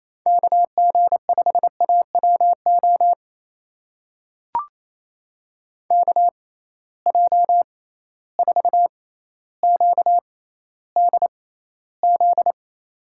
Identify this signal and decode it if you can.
Morse code, heard as a keyed tone, about 28 wpm, spaced as for 14 wpm, 700 Hz. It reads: KG5AWO KJ4QDZ
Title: USA Call Signs - 6 Characters - 6x Character Spacing - 28wpm